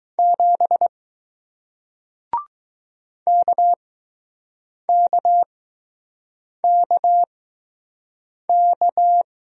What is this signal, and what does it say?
Morse code, heard as a keyed tone, about 21 wpm, spaced as for 6 wpm, 700 Hz. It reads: 7 KKKK